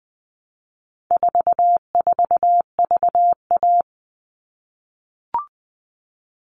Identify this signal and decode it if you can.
Morse code, heard as a keyed tone, about 20 wpm, 700 Hz. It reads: 44VA
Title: All Cut Numbers - 4-Digit 20wpm